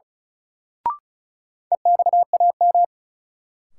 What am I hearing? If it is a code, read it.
Morse code, heard as a keyed tone, about 34 wpm, 700 Hz. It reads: EXAM
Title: Sentences from Top 1000 Words - Review Part 5 - New Word Drill - 35wpm